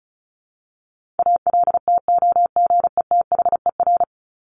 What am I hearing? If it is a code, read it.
Morse code, heard as a keyed tone, about 34 wpm, 700 Hz. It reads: ALTOGETHER
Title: Top 1501-1750 Words 35wpm